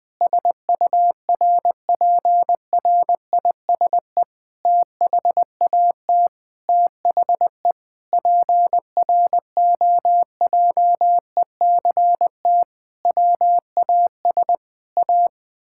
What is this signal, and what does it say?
Morse code, heard as a keyed tone, about 20 wpm, 700 Hz. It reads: SURPRISE THAT THE PROJECT WAS A
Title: Sentences with 8-Word Phrases - 20wpm